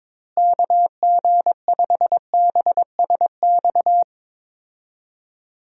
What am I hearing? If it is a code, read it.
Morse code, heard as a keyed tone, about 22 wpm, 700 Hz. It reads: KG5BSX
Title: USA Call Signs - 6 Characters - Repeated 3x - 22wpm